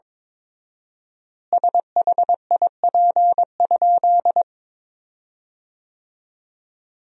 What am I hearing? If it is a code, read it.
Morse code, heard as a keyed tone, about 22 wpm, 700 Hz. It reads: SHIP?
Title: 4x Word-Spacing - Sentences from Top 400 Words - 22wpm